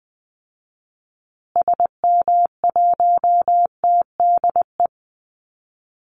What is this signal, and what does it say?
Morse code, heard as a keyed tone, about 19 wpm, 700 Hz. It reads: SM1TDE
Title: Call Signs Repeated 2x - 20wpm